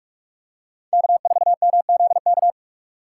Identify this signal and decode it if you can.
Morse code, heard as a keyed tone, about 44 wpm, 700 Hz. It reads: K4MZK
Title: USA Call Signs - 6x Character Spacing - 45wpm